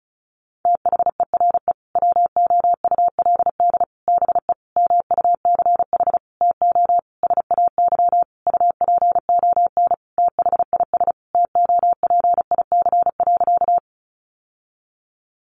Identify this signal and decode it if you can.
Morse code, heard as a keyed tone, about 34 wpm, 700 Hz. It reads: THERE WOULD BE MUCH TO SAY UPON THIS TOPIC.
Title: Sentences from Top 1200 Words 35wpm